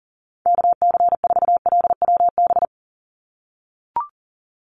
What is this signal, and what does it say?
Morse code, heard as a keyed tone, about 40 wpm, 700 Hz. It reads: KC4LWB
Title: USA Call Signs 40wpm